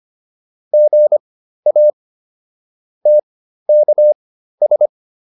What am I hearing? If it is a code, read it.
Morse code, heard as a keyed tone, about 25 wpm, spaced as for 13 wpm, 600 Hz. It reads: GA TKS